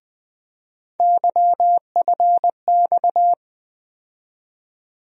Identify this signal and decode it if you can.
Morse code, heard as a keyed tone, about 20 wpm, 700 Hz. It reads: YFX
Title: USA Call Sign Postfixes - 3 Letters 20wpm